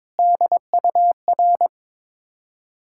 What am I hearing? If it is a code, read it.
Morse code, heard as a keyed tone, about 22 wpm, 690 Hz. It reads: DUR
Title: Ontario QSO Party Exchange 22wpm